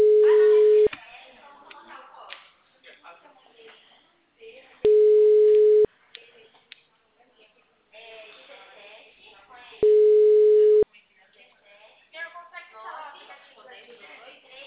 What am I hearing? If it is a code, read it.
Ringback tone (Europe, 425 Hz), with speech.
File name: out- -2003- - - .27